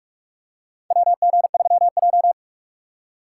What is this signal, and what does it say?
Morse code, heard as a keyed tone, about 43 wpm, 700 Hz. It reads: WG3J
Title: USA Call Signs - 4 Characters - 2x Character Spacing - 45wpm